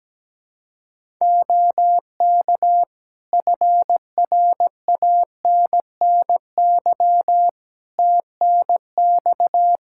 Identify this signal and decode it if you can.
Morse code, heard as a keyed tone, about 17 wpm, 700 Hz. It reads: OK FRANNY TNX
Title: Practice QSO - Exchange 4 - Ending - 17wpm